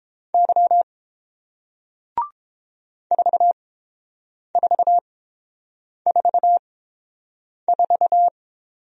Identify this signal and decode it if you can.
Morse code, heard as a keyed tone, about 30 wpm, spaced as for 7 wpm, 700 Hz. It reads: Y 4444